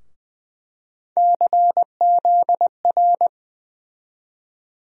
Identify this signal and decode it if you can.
Morse code, heard as a keyed tone, about 20 wpm, 700 Hz. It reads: CZR
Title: USA Call Sign Postfixes - 3 Letters 20wpm